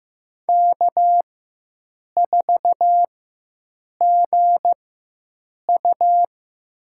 Morse code, heard as a keyed tone, about 16 wpm, spaced as for 7 wpm, 700 Hz. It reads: K4GU
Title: USA Call Signs - 4 Characters - 4x Character Spacing - 15wpm